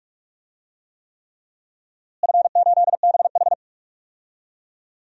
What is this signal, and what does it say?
Morse code, heard as a keyed tone, about 45 wpm, 700 Hz. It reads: F8BH
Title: DXCC-100 Call Signs - 4 Characters - Repeated 3x - 45wpm